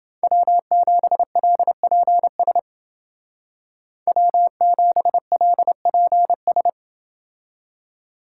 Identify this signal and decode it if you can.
Morse code, heard as a keyed tone, about 28 wpm, 700 Hz. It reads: W7LPS W7LPS